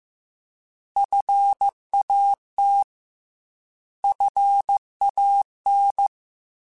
Morse code, heard as a keyed tone, about 15 wpm, 800 Hz. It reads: FAT FAN
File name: Alphabetical / Alphabetical mp3s / Fs @13